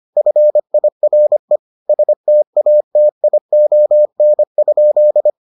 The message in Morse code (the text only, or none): FIRE STATION?